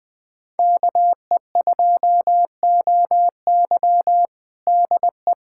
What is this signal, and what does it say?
Morse code, heard as a keyed tone, about 20 wpm, 700 Hz. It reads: KE2OY DE
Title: USA Callsign DE Callsign 20wpm